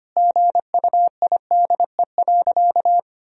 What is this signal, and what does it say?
Morse code, heard as a keyed tone, about 25 wpm, 700 Hz. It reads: GUIDE.
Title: Sentences from Top 1000 Words - Review Part 3 - 25wpm